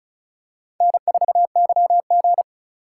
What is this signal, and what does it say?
Morse code, heard as a keyed tone, about 34 wpm, 700 Hz. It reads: N4YG
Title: USA Call Signs - 4x Character Spacing - 35wpm